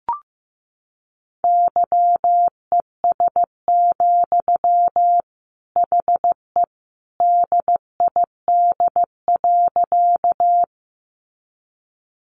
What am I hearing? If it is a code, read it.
Morse code, heard as a keyed tone, about 15 wpm, 700 Hz. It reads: YES, HE DID.
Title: Sentences from Top 500 Words 15wpm